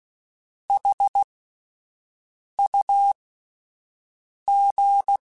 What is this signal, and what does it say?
Morse code, heard as a keyed tone, about 16 wpm, spaced as for 5 wpm, 800 Hz. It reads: HUG